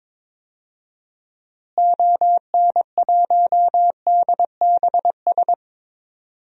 Morse code, heard as a keyed tone, about 22 wpm, 700 Hz. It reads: ON1DBS
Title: DXCC-100 Call Signs - 6 Characters - Repeated 2x - 22wpm